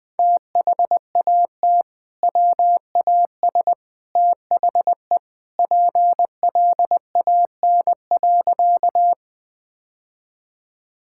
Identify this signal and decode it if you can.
Morse code, heard as a keyed tone, about 20 wpm, 700 Hz. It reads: THAT WAS THE PLAN.
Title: Sentences with 2-Word Phrases - 20wpm